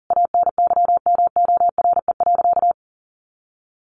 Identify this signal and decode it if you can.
Morse code, heard as a keyed tone, about 37 wpm, 700 Hz. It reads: ANYMORE.
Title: Additional Easy Phrases 40wpm